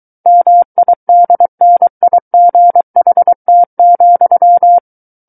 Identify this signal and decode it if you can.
Morse code, heard as a keyed tone, about 23 wpm, 700 Hz. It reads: MIDNIGHT,
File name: Book One / 14-Chapter 14 - 23 WPM